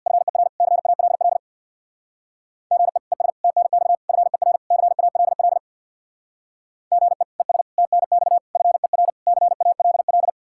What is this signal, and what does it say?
Morse code, heard as a keyed tone, about 67 wpm, 700 Hz. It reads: FER CALL GE ES TNX FER CALL GE ES TNX FER CALL